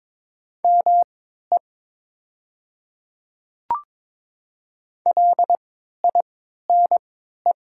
Morse code, heard as a keyed tone, about 22 wpm, spaced as for 13 wpm, 700 Hz. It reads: ME LINE
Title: 3x Character Spacing - Top 200 Words - 22wpm